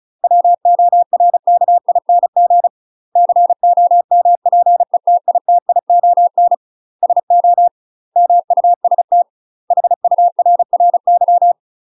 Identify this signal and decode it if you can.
Morse code, heard as a keyed tone, about 34 wpm, 700 Hz. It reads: WORKING COMPETITION SO MUST HURRY
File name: UK MASTER QSOs TRAINING / 35WPM / UK MASTER QSOs 0037